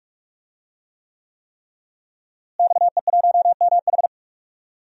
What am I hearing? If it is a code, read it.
Morse code, heard as a keyed tone, about 43 wpm, 700 Hz. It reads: XE1MH
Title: DXCC-100 Call Signs - 5 Characters - Repeated 3x - 45wpm